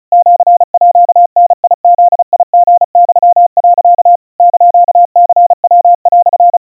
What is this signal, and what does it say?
Morse code (Japanese Wabun code), heard as a keyed tone, about 34 wpm, 700 Hz. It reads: シーダブリユ、 <KN>ニヤ)
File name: kawa / Morse sample / aiueo 120 cpm 700 Hz Crispy